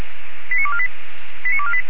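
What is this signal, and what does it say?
Five-tone selective call (ZVEI), heard as quick runs of five tones, tones of about 70 ms. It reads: 89248 89248
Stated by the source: ZVEI five-tone selective call received on SDR, from the Feuerwehr Orlamünde fire brigade server.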